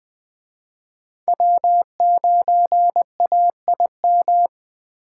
Morse code, heard as a keyed tone, about 20 wpm, 700 Hz. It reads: W9AIM
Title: Speed Racing - USA 1x3 Call Signs 17wpm